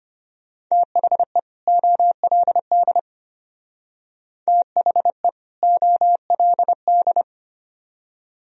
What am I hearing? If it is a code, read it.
Morse code, heard as a keyed tone, about 31 wpm, 700 Hz. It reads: THE OLD THE OLD
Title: Speed-Racing - Sets of 2 Words 25wpm